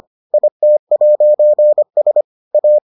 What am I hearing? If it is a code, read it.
Morse code, heard as a keyed tone, about 25 wpm, 600 Hz. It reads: IT'S A